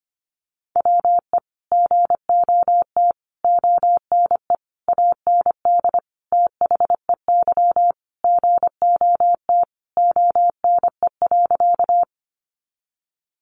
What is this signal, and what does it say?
Morse code, heard as a keyed tone, about 25 wpm, 700 Hz. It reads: WE GOT ONE AND THEY GOT ONE.
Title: Sentences from Top 300 Words 25wpm